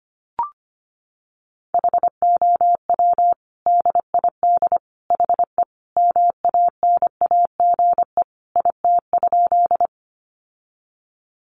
Morse code, heard as a keyed tone, about 25 wpm, 700 Hz. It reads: HOW DID HE MANAGE IT?
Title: Sentences from Top 1400 Words 25wpm